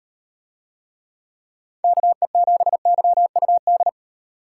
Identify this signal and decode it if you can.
Morse code, heard as a keyed tone, about 36 wpm, 700 Hz. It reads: KE7YUD